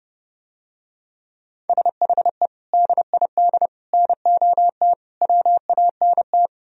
Morse code, heard as a keyed tone, about 32 wpm, 700 Hz. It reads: SHE DID NOT WANT